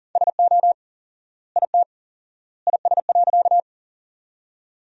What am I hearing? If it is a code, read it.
Morse code, heard as a keyed tone, about 40 wpm, 700 Hz. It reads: SO IT IS.
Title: 4x Word-Spacing - Sentences from Top 100 Words - 40wpm